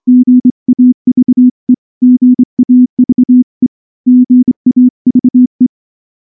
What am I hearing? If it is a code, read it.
Morse code, heard as a keyed tone, about 25 wpm, 250 Hz. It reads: GAVE GAVE GAVE
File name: morse-training-25WPM